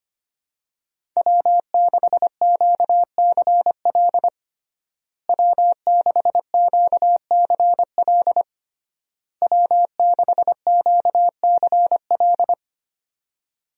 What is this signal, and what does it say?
Morse code, heard as a keyed tone, about 25 wpm, 700 Hz. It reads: W6QCL W6QCL W6QCL
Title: USA Call Signs - 5 Characters - Repeated 3x - 25wpm